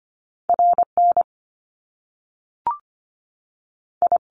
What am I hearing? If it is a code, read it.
Morse code, heard as a keyed tone, about 25 wpm, spaced as for 3 wpm, 700 Hz. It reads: R NI
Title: Top Two Letter Combinations in Words 25wpm